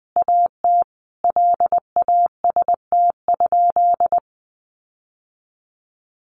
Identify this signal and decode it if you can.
Morse code, heard as a keyed tone, about 20 wpm, 700 Hz. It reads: AT LAST?
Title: Sentences from Top 1300 Words 20wpm